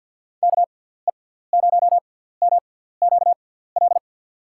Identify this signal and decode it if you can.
Morse code, heard as a keyed tone, about 50 wpm, spaced as for 17 wpm, 700 Hz. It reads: KE0MQL
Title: USA Call Signs - 6 Characters - 6x Character Spacing - 50wpm